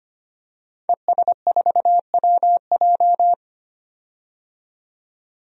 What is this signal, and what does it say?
Morse code, heard as a keyed tone, about 25 wpm, 700 Hz. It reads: ES4WJ
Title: DXCC-100 Call Signs - 5 Characters - 25wpm